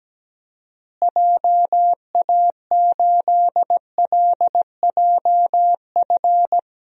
Morse code, heard as a keyed tone, about 17 wpm, 700 Hz. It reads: JA8LJF
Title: DXCC-100 Call Signs - 6 Characters - Repeated 3x - 17wpm